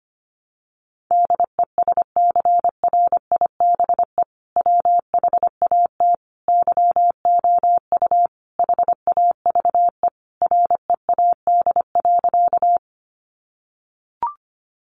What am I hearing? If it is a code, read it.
Morse code, heard as a keyed tone, about 25 wpm, 700 Hz. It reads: DESCRIBE WHAT YOU HAVE READ.
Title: Sentences from Top 1000 Words 25wpm